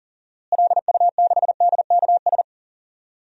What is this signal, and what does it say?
Morse code, heard as a keyed tone, about 39 wpm, 700 Hz. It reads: LU6DKS
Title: DXCC-100 Call Signs - 6 Characters - Repeated 3x - 40wpm